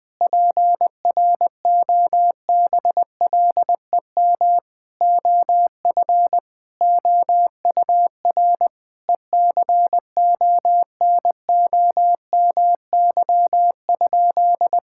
Morse code, heard as a keyed tone, about 20 wpm, 700 Hz. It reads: PROBLEM OF OUR ECONOMY?
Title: Sentences from Top 1000 Words - Review Part 5 - 20wpm